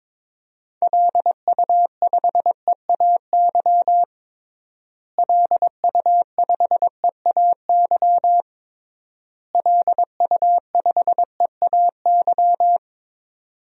Morse code, heard as a keyed tone, about 22 wpm, 700 Hz. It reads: LU5EAY LU5EAY LU5EAY
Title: DXCC-100 Call Signs - 6 Characters - Repeated 3x - 22wpm